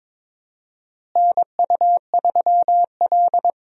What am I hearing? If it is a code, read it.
Morse code, heard as a keyed tone, about 22 wpm, 700 Hz. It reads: NU3L
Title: USA Call Signs - 4 Characters - Repeated 3x - 22wpm